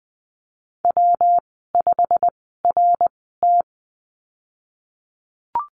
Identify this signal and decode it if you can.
Morse code, heard as a keyed tone, about 19 wpm, spaced as for 14 wpm, 700 Hz. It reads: W5RT
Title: 207 Lesson 33g - Letter j - Call Signs 20wpm